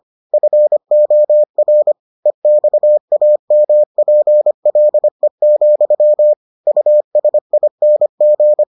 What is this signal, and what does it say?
Morse code, heard as a keyed tone, about 25 wpm, 600 Hz. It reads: FOR EXAMPLE, USING